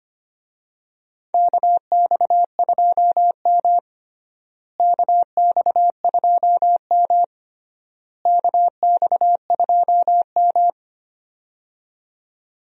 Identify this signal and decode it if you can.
Morse code, heard as a keyed tone, about 25 wpm, 700 Hz. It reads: KX2M KX2M KX2M
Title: USA Call Signs - 4 Characters - Repeated 3x - 25wpm